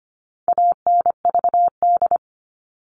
Morse code, heard as a keyed tone, about 25 wpm, 700 Hz. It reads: ANVD
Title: All Cut Numbers - 4-Digit 25wpm